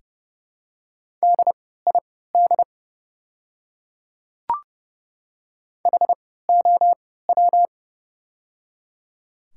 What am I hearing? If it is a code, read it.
Morse code, heard as a keyed tone, about 30 wpm, spaced as for 17 wpm, 700 Hz. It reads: DID HOW
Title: 3x Character Spacing - Top 100 Words - 30wpm